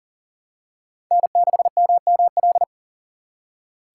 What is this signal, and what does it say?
Morse code, heard as a keyed tone, about 40 wpm, 700 Hz. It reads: N6MML
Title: USA Call Signs - 5 Characters - 2x Character Spacing - 40wpm